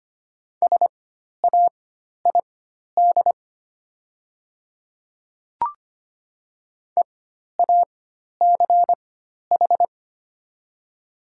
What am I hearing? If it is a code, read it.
Morse code, heard as a keyed tone, about 25 wpm, spaced as for 12 wpm, 690 Hz. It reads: SAID EACH
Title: 4x Character Spacing - Top 200 Words - 25wpm